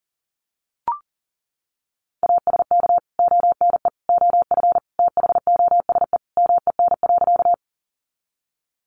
Morse code, heard as a keyed tone, about 40 wpm, 700 Hz. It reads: ASK ONE OF THOSE MEN.
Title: Sentences from Top 300 Words 40wpm